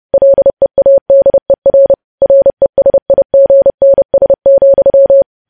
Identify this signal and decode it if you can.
Morse code, heard as a keyed tone, about 30 wpm, 550 Hz. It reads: LEADER RESIGNS,
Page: News Headlines in Morse Code at 30 WPM